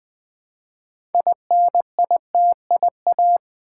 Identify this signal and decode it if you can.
Morse code, heard as a keyed tone, about 20 wpm, 700 Hz. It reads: INITIA
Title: Build A Word - 2 Letters At A Time - 8-Letter Words - 20wpm